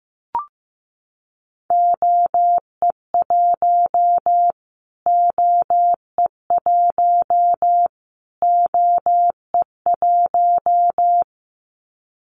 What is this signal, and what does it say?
Morse code, heard as a keyed tone, about 15 wpm, 700 Hz. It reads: OE1 OE1 OE1
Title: Callsign Prefixes for Top 25 DXCC - Repeated 3x - 15wpm